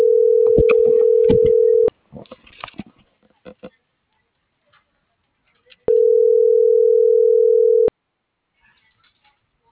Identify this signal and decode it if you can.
Ringback tone (North America), 440 + 480 Hz.